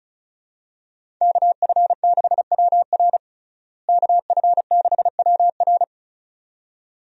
Morse code, heard as a keyed tone, about 36 wpm, 700 Hz. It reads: KF6WR KF6WR
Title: USA Call Signs - 5 Characters - Repeated 2x - 35wpm